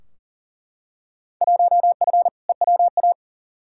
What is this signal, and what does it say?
Morse code, heard as a keyed tone, about 40 wpm, 700 Hz. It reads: 1F EWA
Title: Field Day Exchanges Encore - 40wpm